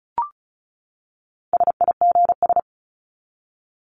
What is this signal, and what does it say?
Morse code, heard as a keyed tone, about 34 wpm, 700 Hz. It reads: SIGS
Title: Abbreviations 35wpm